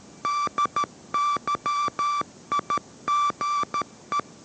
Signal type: Morse code